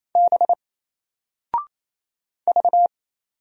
Morse code, heard as a keyed tone, about 27 wpm, 700 Hz. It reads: B V
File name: Single-Letter-Number-28wpm